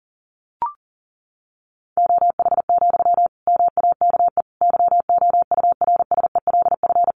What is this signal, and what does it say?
Morse code, heard as a keyed tone, about 40 wpm, 700 Hz. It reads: OH, MAKE YOURSELF
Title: Sentences from Top 700 Words 40wpm